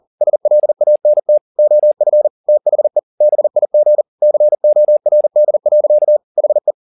Morse code, heard as a keyed tone, about 40 wpm, 600 Hz. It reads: SLANT OF THE BIG CORD. HE
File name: the old man and the sea 2 40wpm